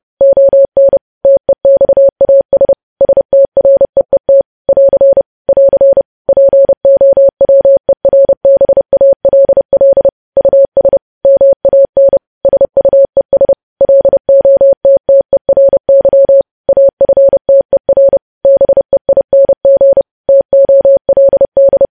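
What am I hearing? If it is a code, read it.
Morse code, heard as a keyed tone, about 30 wpm, 550 Hz. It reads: ON TEXAS STREET <AR> <AR> POWERBALL US MAN SUES LOTTERY AFTER BEING TOLD
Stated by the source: Morse news headlines sent at 30.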